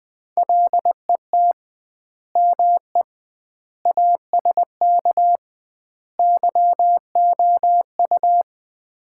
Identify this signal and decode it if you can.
Morse code, heard as a keyed tone, about 20 wpm, 700 Hz. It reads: LET ME ASK YOU
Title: 2x Word-Spacing - Sets of 4 Words - 20wpm